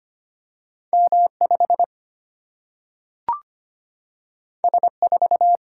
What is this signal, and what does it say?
Morse code, heard as a keyed tone, about 25 wpm, 700 Hz. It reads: M5 S4
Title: Single-Letter Single-Number 25wpm